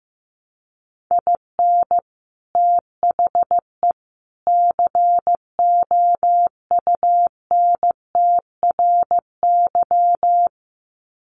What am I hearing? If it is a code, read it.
Morse code, heard as a keyed tone, about 14 wpm, 700 Hz. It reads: IN THE COUNTRY
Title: Sets of 3 Words 15wpm